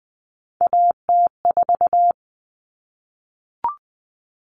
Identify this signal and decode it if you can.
Morse code, heard as a keyed tone, about 20 wpm, 700 Hz. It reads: AT4
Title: Common Cut Numbers - 3-Digit 20wpm